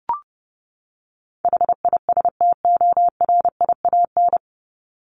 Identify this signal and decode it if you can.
Morse code, heard as a keyed tone, about 30 wpm, 700 Hz. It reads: HISTORIAN
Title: Top 1751-2000 Words 30wpm